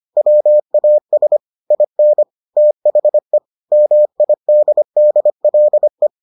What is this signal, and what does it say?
Morse code, heard as a keyed tone, about 25 wpm, 600 Hz. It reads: WAS IN THE MIDDLE